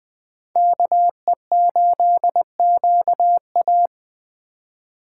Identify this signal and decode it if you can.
Morse code, heard as a keyed tone, about 19 wpm, 700 Hz. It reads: KE8QA